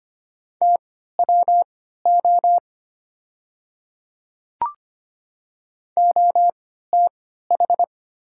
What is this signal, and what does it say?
Morse code, heard as a keyed tone, about 25 wpm, spaced as for 14 wpm, 700 Hz. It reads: TWO OTH
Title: 3x Character Spacing - Top 100 Words - 25wpm